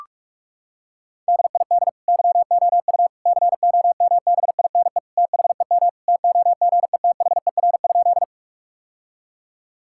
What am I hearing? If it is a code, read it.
Morse code, heard as a keyed tone, about 42 wpm, 700 Hz. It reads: DID YOU COMBINE THEM TOGETHER?
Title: Sentences from Top 1600 Words 45wpm